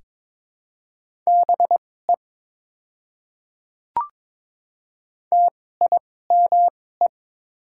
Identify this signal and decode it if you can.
Morse code, heard as a keyed tone, about 22 wpm, spaced as for 16 wpm, 700 Hz. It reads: BE TIME
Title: 2x Character Spacing - Top 100 Words - 22wpm